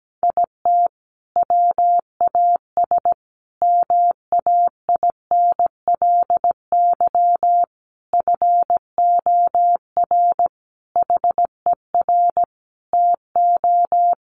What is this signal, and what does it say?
Morse code, heard as a keyed tone, about 17 wpm, 700 Hz. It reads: IT WAS MAINLY FOR HER TO
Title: Sentences from Top 1400 Words 17wpm